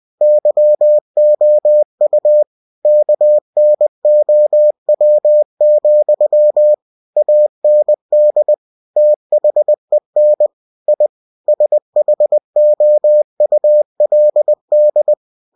Morse code, heard as a keyed tone, about 20 wpm, 600 Hz. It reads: YOU KNOW, AND THEN I SHOULD
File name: cw-books / winnie-the-pooh / winnie-20wpm-page-0036